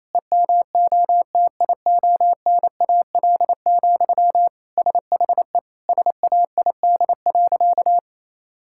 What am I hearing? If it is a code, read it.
Morse code, heard as a keyed tone, about 28 wpm, 700 Hz. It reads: EMOTIONAL, SHE SAID.
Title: Sentences from Top 1000 Words - Review Part 5 - 28wpm